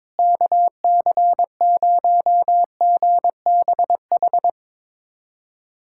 Morse code, heard as a keyed tone, about 21 wpm, 700 Hz. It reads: KC0GBH
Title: USA Callsign DE Callsign 22wpm